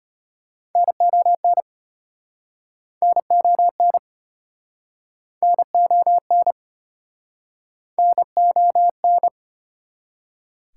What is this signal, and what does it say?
Morse code, heard as a keyed tone, about 36 wpm, 700 Hz. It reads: NON NON NON NON